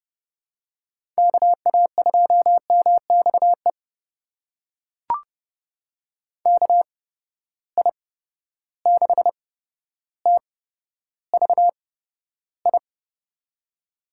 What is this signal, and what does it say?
Morse code, heard as a keyed tone, about 30 wpm, spaced as for 13 wpm, 700 Hz. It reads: KA2MXE KI6TVI
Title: USA Call Signs - 6 Characters - 8x Character Spacing - 30wpm